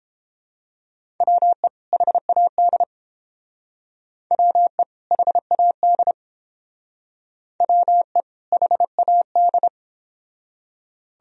Morse code, heard as a keyed tone, about 29 wpm, 700 Hz. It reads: WE HAD WE HAD WE HAD